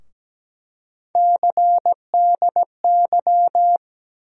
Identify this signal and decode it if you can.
Morse code, heard as a keyed tone, about 17 wpm, 700 Hz. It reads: CDY
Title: USA Call Sign Postfixes - 3 Letters 17wpm